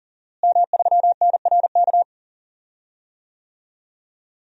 Morse code, heard as a keyed tone, about 40 wpm, 700 Hz. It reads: M3NRK
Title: DXCC-100 Call Signs - 5 Characters - Repeated 3x - 40wpm